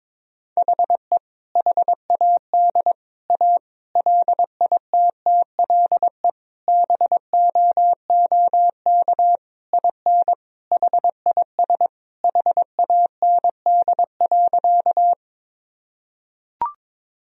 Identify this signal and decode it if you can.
Morse code, heard as a keyed tone, about 22 wpm, 700 Hz. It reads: HE HAD A LITTLE BOOK IN HIS HAND.
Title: Sentences from Top 1000 Words - Review Part 1 - 22wpm